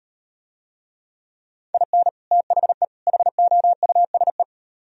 Morse code, heard as a keyed tone, about 38 wpm, 700 Hz. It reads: IN THE HOUSE